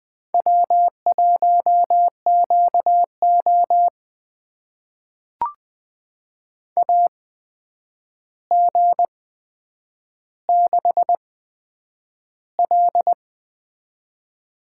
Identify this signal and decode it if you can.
Morse code, heard as a keyed tone, about 20 wpm, spaced as for 9 wpm, 700 Hz. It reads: W1QO AG6L
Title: USA Call Signs - 4 Characters - 8x Character Spacing - 20wpm